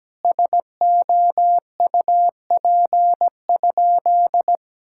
Morse code, heard as a keyed tone, about 17 wpm, 700 Hz. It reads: SOUP?
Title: Sentences from Top 1000 Words - Review Part 6 - 17wpm